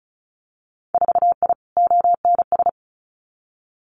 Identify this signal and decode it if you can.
Morse code, heard as a keyed tone, about 36 wpm, 700 Hz. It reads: 4I ONS
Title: Winter Field Day Exchanges Encore 35wpm